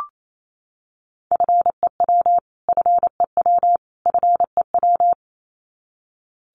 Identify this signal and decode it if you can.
Morse code, heard as a keyed tone, about 28 wpm, 700 Hz. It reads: FEW FEW FEW
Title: Top 500 Words Repeated 28wpm